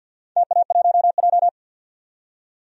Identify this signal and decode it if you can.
Morse code, heard as a keyed tone, about 52 wpm, 700 Hz. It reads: TA1J
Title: DXCC-100 Call Signs - 4 Characters - 50wpm